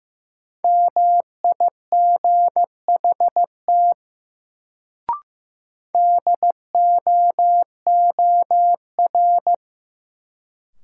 Morse code, heard as a keyed tone, about 15 wpm, 700 Hz. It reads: MIGHT DOOR
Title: Sentences from Top 300 Words - New Word Drill - 15wpm